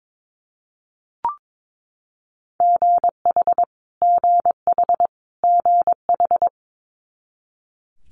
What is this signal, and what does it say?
Morse code, heard as a keyed tone, about 22 wpm, 700 Hz. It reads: GH GH GH
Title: Multi-letter Phonemes Repeated 3x 22wpm